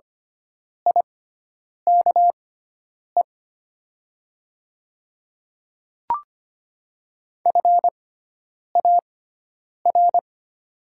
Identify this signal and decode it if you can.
Morse code, heard as a keyed tone, about 25 wpm, spaced as for 9 wpm, 700 Hz. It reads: IKE FAR
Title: 6x Character Spacing - Top 300 Words - 25wpm